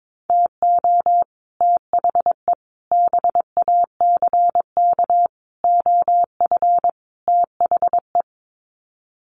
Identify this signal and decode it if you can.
Morse code, heard as a keyed tone, about 22 wpm, 700 Hz. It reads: TO THE BACK OF THE
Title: Sets of 5 Words 22wpm